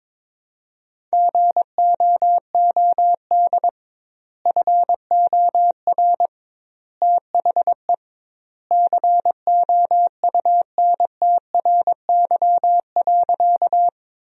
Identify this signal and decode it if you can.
Morse code, heard as a keyed tone, about 22 wpm, 700 Hz. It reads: GOOD FOR THE COUNTRY.
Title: 2x Word-Spacing - Sentences from Top 300 Words - 22wpm